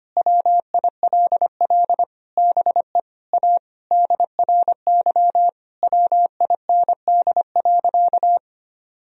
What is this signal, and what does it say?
Morse code, heard as a keyed tone, about 25 wpm, 700 Hz. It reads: WILL BE A DRY WIND.
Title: Sentences from Top 1000 Words - Review Part 3 - 25wpm